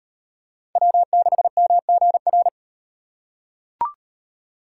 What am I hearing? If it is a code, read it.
Morse code, heard as a keyed tone, about 35 wpm, 700 Hz. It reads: W6MGR